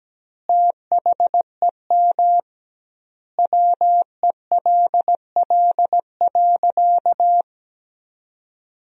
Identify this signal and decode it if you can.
Morse code, heard as a keyed tone, about 17 wpm, 700 Hz. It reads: THEM WELL.
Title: 2x Word-Spacing - Sentences from Top 300 Words - 17wpm